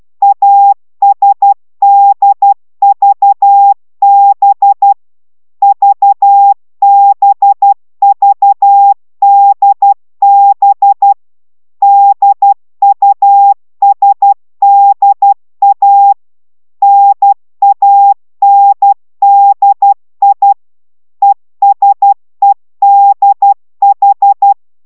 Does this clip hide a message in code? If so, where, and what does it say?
Yes: Morse code from 0.2 to 24.5 s, heard as a keyed tone, about 12 wpm, 800 Hz. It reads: ASDVB VBVDB DUSDA NANDI ESEDH